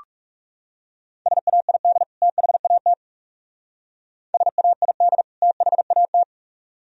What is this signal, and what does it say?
Morse code, heard as a keyed tone, about 39 wpm, 700 Hz. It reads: SAID THAT SAID THAT